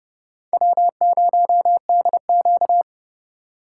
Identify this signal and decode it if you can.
Morse code, heard as a keyed tone, about 29 wpm, 700 Hz. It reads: W0DQ